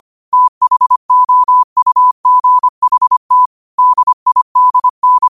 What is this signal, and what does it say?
Morse code, heard as a keyed tone, about 25 wpm, 1000 Hz. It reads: THOUGHT DIDN
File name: qotdfiles / 25wpm